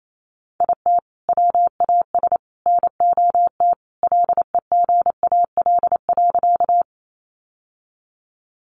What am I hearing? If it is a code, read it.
Morse code, heard as a keyed tone, about 28 wpm, 700 Hz. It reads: IT WAS NOT LEGAL.
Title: Sentences from Top 1200 Words 28wpm